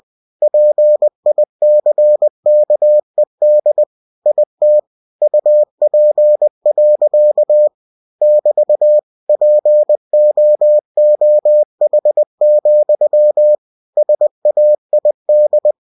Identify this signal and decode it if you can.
Morse code, heard as a keyed tone, about 20 wpm, 600 Hz. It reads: PICKED IT UP. <BT> POOH, SAID
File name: cw-books / winnie-the-pooh / winnie-20wpm-page-0087